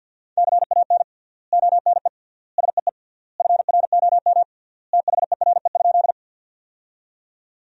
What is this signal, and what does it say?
Morse code, heard as a keyed tone, about 50 wpm, 700 Hz. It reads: CAN ONE SEE FROM THERE?
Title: 3x Word-Spacing - Sentences from Top 100 Words - 50wpm